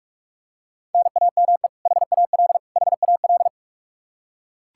Morse code, heard as a keyed tone, about 43 wpm, 700 Hz. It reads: NAME HAL HAL
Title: Practice QSO - Exchange 1 - Beginning - 45wpm